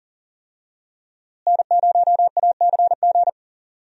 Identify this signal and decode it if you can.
Morse code, heard as a keyed tone, about 40 wpm, 700 Hz. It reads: N0ACG
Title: USA Call Signs - 5 Characters - Repeated 3x - 40wpm